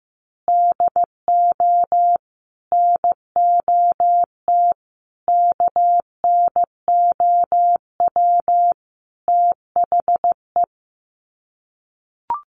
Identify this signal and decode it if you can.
Morse code, heard as a keyed tone, about 15 wpm, 700 Hz. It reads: DO NOT KNOW THE
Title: Sets of 5 Words 15wpm